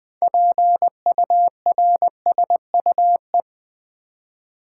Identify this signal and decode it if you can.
Morse code, heard as a keyed tone, about 20 wpm, 700 Hz. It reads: PURSUE
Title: Build A Word - 3 Letters At A Time - 6-Letter Words - 20wpm